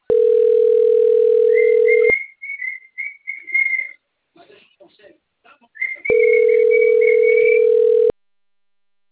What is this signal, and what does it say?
Ringback tone (North America), 440 + 480 Hz.
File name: out- - - - - .80859